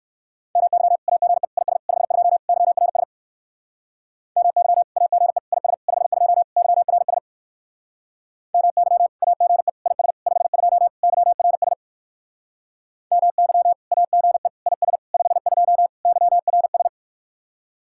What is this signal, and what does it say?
Morse code, heard as a keyed tone, about 60 wpm, 700 Hz. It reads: MY AGE IS 52 YRS MY AGE IS 52 YRS MY AGE IS 52 YRS MY AGE IS 52 YRS